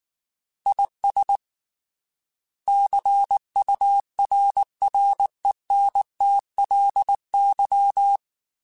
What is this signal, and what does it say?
Morse code, heard as a keyed tone, about 19 wpm, 800 Hz. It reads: IS CURRENTLY
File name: Weather Sets / Weather Sets-mp3s / Weather 5 1 @18